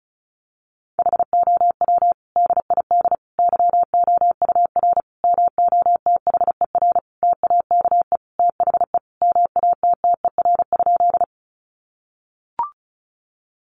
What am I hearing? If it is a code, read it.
Morse code, heard as a keyed tone, about 34 wpm, 700 Hz. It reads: HOW DID YOUR MOTHER TAKE THE MATTER?
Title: Sentences from Top 700 Words 35wpm